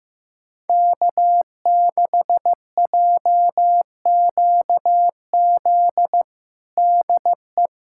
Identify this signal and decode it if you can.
Morse code, heard as a keyed tone, about 15 wpm, 700 Hz. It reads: K6JQZ DE
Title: USA Callsign DE Callsign - 5 Characters - 15wpm